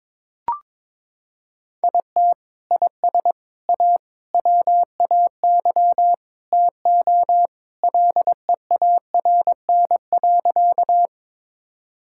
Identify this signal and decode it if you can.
Morse code, heard as a keyed tone, about 22 wpm, 700 Hz. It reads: IT IS A WAY TO LEARN.
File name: practice / 22wpm / Sentences with 3-Word Phrases - 22wpm